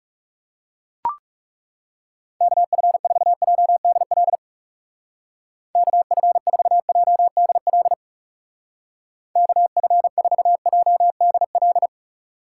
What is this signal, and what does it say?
Morse code, heard as a keyed tone, about 39 wpm, 700 Hz. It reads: KF4JDL KF4JDL KF4JDL